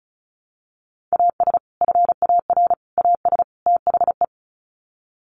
Morse code, heard as a keyed tone, about 34 wpm, 700 Hz. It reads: AS FAR AS THE
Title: Sets of 4 Words 35wpm